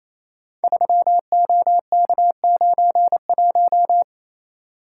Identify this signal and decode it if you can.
Morse code, heard as a keyed tone, about 29 wpm, 700 Hz. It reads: 3OK91
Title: Code Groups - 5 Letter:Numbers - 28wpm